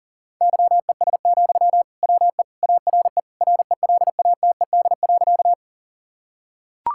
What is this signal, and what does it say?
Morse code, heard as a keyed tone, about 40 wpm, 700 Hz. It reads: YES, WE ARE RELATED.
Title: Sentences from Top 1000 Words - Review Part 5 - 40wpm